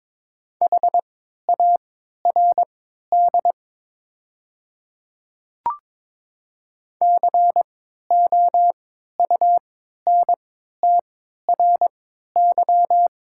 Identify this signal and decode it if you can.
Morse code, heard as a keyed tone, about 22 wpm, spaced as for 12 wpm, 700 Hz. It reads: HARD COUNTRY
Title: 3x Character Spacing - Top 300 Words - 22wpm